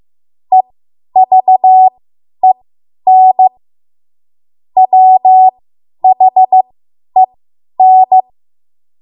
Morse code, heard as a keyed tone, about 15 wpm, spaced as for 10 wpm, 750 Hz. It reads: EVEN WHEN
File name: Morse / 10WPM